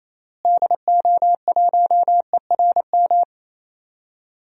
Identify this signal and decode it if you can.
Morse code, heard as a keyed tone, about 28 wpm, 700 Hz. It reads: DO1ERM
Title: DXCC-100 Call Signs - 6 Characters - Repeated 3x - 28wpm